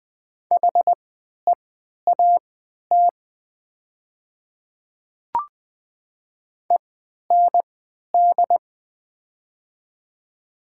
Morse code, heard as a keyed tone, about 20 wpm, spaced as for 11 wpm, 700 Hz. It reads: HEAT END
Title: 3x Character Spacing - Top 500 Words - 20wpm